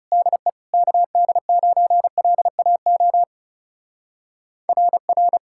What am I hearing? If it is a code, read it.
Morse code, heard as a keyed tone, about 36 wpm, 700 Hz. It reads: DE KD9LAO RR
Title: Speed-Racing Common QSO Phrases Encore - 30wpm